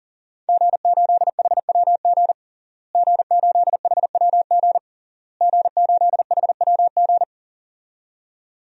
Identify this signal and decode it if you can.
Morse code, heard as a keyed tone, about 40 wpm, 700 Hz. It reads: G8HWG G8HWG G8HWG
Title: DXCC-100 Call Signs - 5 Characters - Repeated 3x - 40wpm